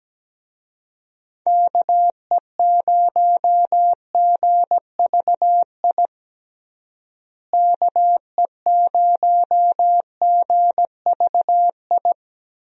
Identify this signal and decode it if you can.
Morse code, heard as a keyed tone, about 17 wpm, 700 Hz. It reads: KE0GVI KE0GVI
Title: USA Call Signs - 6 Characters - Repeated 2x - 17wpm